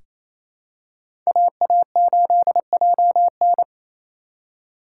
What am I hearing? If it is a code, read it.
Morse code, heard as a keyed tone, about 28 wpm, 700 Hz. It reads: AA8JN
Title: USA Call Signs - 5 Characters - 6x Character Spacing - 28wpm